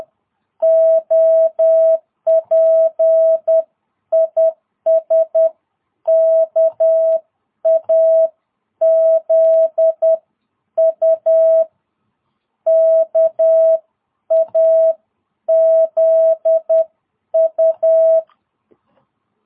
Morse code, heard as a keyed tone, about 10 wpm, spaced as for 9 wpm, 660 Hz. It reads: OP IS KAZU KAZU
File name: jh8cbh / 2022cw